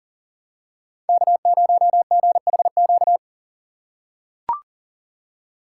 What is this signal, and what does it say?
Morse code, heard as a keyed tone, about 40 wpm, 700 Hz. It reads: K0GHQ
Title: USA Call Signs - 5 Characters - 3x Character Spacing - 40wpm